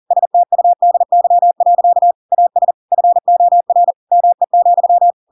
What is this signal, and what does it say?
Morse code, heard as a keyed tone, about 40 wpm, 700 Hz. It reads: STUDY. AS FOR ME,
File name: Pinocchio / Pinocchio ch4 40 40 ik1hge